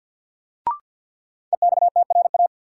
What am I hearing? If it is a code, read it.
Morse code, heard as a keyed tone, about 53 wpm, 700 Hz. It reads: EXTRA
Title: Sentences from Top 1000 Words - New Word Drill - 50wpm